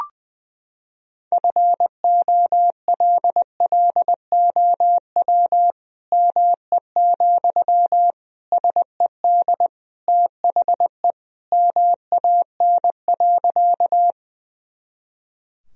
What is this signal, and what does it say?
Morse code, heard as a keyed tone, about 20 wpm, 700 Hz. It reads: FOLLOW ME, SED THE MAN.
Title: Sentences from Top 200 Words and Abbreviations 20wpm